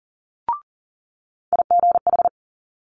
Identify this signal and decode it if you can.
Morse code, heard as a keyed tone, about 37 wpm, 700 Hz. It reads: IGH
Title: Multi-letter Phonemes Quick Fire 40wpm